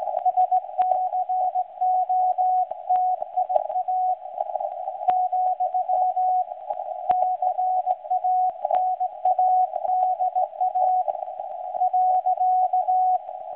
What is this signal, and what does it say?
Morse code, heard as a keyed tone, about 17 wpm, 740 Hz. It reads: SCOTIA , CANADA .
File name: WebSDR records / SAQ recording start 17.2kHz